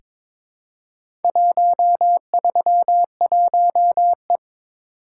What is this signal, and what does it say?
Morse code, heard as a keyed tone, about 23 wpm, 700 Hz. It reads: 131E
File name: practice / 22wpm / Four Letter-Number Combinations - 22wpm